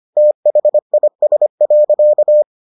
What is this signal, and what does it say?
Morse code, heard as a keyed tone, about 25 wpm, 600 Hz. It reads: THIS.